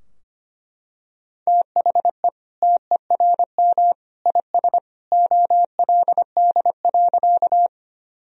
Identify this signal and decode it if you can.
Morse code, heard as a keyed tone, about 25 wpm, 700 Hz. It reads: THE TERM IS OLD.
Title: Sentences from Top 1000 Words - Review Part 4 - 25wpm